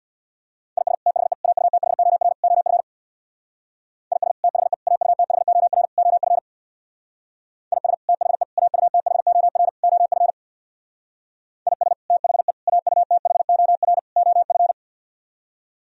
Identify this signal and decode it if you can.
Morse code, heard as a keyed tone, about 64 wpm, 700 Hz. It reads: IS THE AUTHOR OF IS THE AUTHOR OF IS THE AUTHOR OF IS THE AUTHOR OF